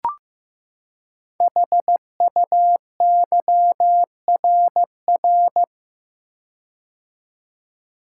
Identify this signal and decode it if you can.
Morse code, heard as a keyed tone, about 15 wpm, 700 Hz. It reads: HUYRR
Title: Code Groups - 5 Letters - 15wpm